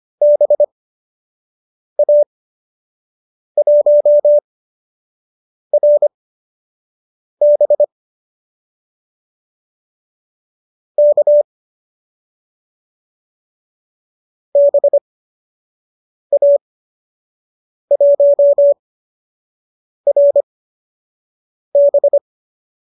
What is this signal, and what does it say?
Morse code, heard as a keyed tone, about 25 wpm, spaced as for 6 wpm, 590 Hz. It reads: BA1RB K BA1RB
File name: qso002 06